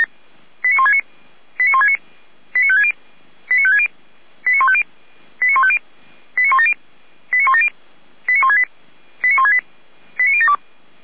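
Five-tone selective call (ZVEI), heard as quick runs of five tones, tones of about 70 ms. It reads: 89169 89169 89560 89560 89150 89150 89170 89170 89168 89168 89072
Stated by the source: ZVEI five-tone selective call received by SDR, from the audio archive of Feuerwehr Orlamünde (fire brigade).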